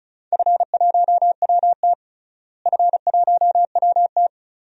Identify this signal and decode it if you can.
Morse code, heard as a keyed tone, about 34 wpm, 700 Hz. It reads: F1WT F1WT
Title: DXCC-100 Call Signs - 4 Characters - Repeated 3x - 35wpm